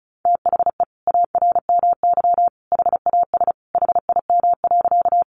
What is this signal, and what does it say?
Morse code, heard as a keyed tone, about 34 wpm, 700 Hz. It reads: THE ARMY HAS HIM.
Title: Sentences from Top 800 Words 35wpm